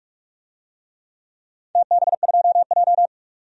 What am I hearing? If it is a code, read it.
Morse code, heard as a keyed tone, about 46 wpm, 700 Hz. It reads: TB2J
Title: DXCC-100 Call Signs - 4 Characters - 45wpm